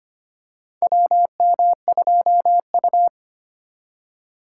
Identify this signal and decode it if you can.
Morse code, heard as a keyed tone, about 25 wpm, 700 Hz. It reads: WM2U
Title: USA Call Signs - 3x Character Spacing - 25wpm